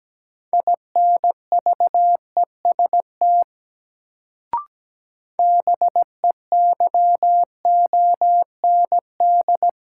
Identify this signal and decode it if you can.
Morse code, heard as a keyed tone, about 17 wpm, 700 Hz. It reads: INVEST BEYOND
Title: Sentences from Top 1100 Words - New Word Drill Alt - 17wpm